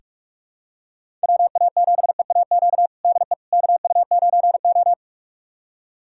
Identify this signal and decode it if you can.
Morse code, heard as a keyed tone, about 46 wpm, 700 Hz. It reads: WA7EAQ DE KU9O
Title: USA Callsign DE Callsign 45wpm